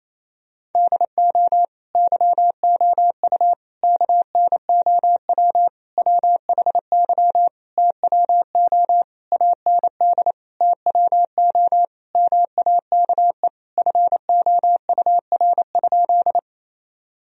Morse code, heard as a keyed tone, about 28 wpm, 700 Hz. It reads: DO YOU KNOW WHY TWO AND TWO MAKE FOUR?
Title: Sentences from Top 1000 Words - Review Part 1 - 28wpm